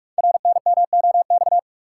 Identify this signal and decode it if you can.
Morse code, heard as a keyed tone, about 44 wpm, 700 Hz. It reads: RNMOX
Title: Code Groups - 5 Letters - 45wpm